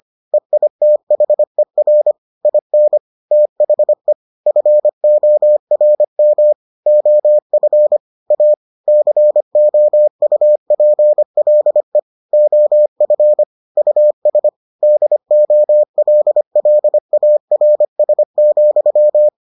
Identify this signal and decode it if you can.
Morse code, heard as a keyed tone, about 25 wpm, 600 Hz. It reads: EITHER IN THE FORM OF A COUPLE OF US DOLLARS,